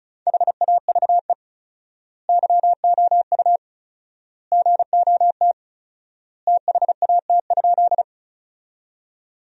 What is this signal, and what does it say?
Morse code, heard as a keyed tone, about 34 wpm, 700 Hz. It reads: HAVE YOU GOT THAT?
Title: 4x Word-Spacing - Sentences from Top 300 Words - 35wpm